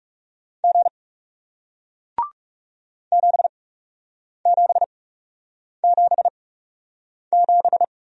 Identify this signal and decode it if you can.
Morse code, heard as a keyed tone, about 41 wpm, spaced as for 8 wpm, 700 Hz. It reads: G 7777